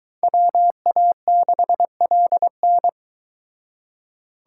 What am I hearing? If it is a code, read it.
Morse code, heard as a keyed tone, about 23 wpm, 700 Hz. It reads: WA6LN